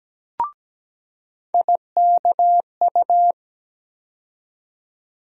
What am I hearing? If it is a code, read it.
Morse code, heard as a keyed tone, about 17 wpm, 700 Hz. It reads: IKU
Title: USA Call Sign Postfixes - 3 Letters 17wpm